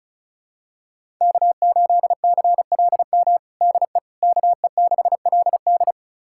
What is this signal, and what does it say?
Morse code, heard as a keyed tone, about 36 wpm, 700 Hz. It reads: K8CLM DE KE6LD
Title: USA Callsign DE Callsign - 5 Characters - 35wpm